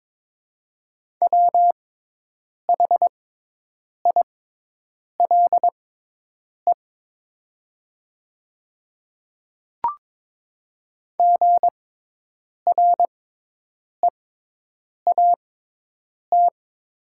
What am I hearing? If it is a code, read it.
Morse code, heard as a keyed tone, about 22 wpm, spaced as for 8 wpm, 700 Hz. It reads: WHILE GREAT
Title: 6x Character Spacing - Top 300 Words - 22wpm